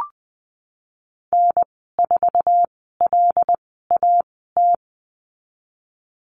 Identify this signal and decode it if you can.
Morse code, heard as a keyed tone, about 20 wpm, spaced as for 15 wpm, 700 Hz. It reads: N4LAT
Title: 128 Lesson 21g - Number 6 - Call Signs 20wpm